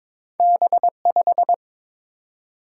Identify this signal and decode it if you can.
Morse code, heard as a keyed tone, about 22 wpm, 700 Hz. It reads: B5
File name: practice / 22wpm / Two Letter-Number Combinations - 22wpm